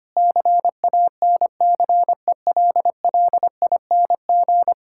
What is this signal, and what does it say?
Morse code, heard as a keyed tone, about 25 wpm, 700 Hz. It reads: CANCELLING
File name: practice / 25wpm / Difficult Words with Hints - 25wpm